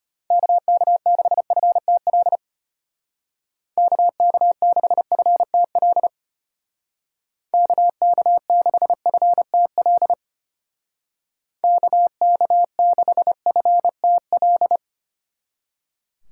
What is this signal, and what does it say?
Morse code, heard as a keyed tone, about 31 wpm, 700 Hz. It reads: KK6FTL KK6FTL KK6FTL KK6FTL
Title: Speed Racing - USA 2x3 Call Signs 25wpm